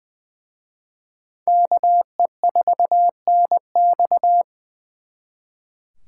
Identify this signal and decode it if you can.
Morse code, heard as a keyed tone, about 24 wpm, 700 Hz. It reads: KE4NX
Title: Speed Racing - USA Call Signs 20wpm